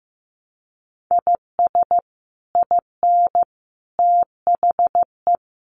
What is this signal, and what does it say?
Morse code, heard as a keyed tone, about 14 wpm, 700 Hz. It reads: IS IN THE
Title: Sets of 5 Words Encore2 15wpm